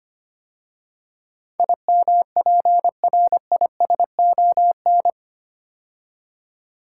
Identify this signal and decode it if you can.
Morse code, heard as a keyed tone, about 25 wpm, 700 Hz. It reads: IMPRISON
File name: 8 Letter Words - First 3 Letters Given - 25wpm